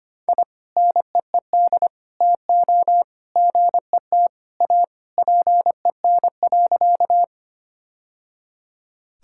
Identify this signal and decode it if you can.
Morse code, heard as a keyed tone, about 25 wpm, 700 Hz. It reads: I NEED TO GET A PEN.
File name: practice / 25wpm / Sentences with 2-Word Phrases - 25wpm